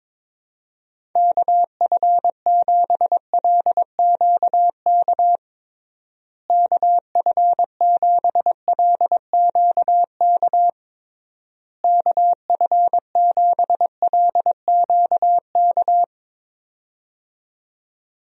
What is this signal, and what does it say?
Morse code, heard as a keyed tone, about 22 wpm, 700 Hz. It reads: KF7LQK KF7LQK KF7LQK
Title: USA Call Signs - 6 Characters - Repeated 3x - 22wpm